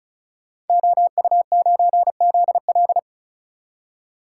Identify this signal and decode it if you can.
Morse code, heard as a keyed tone, about 36 wpm, 700 Hz. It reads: OU9ZL
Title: Code Groups - 5 Letter:Numbers - 35wpm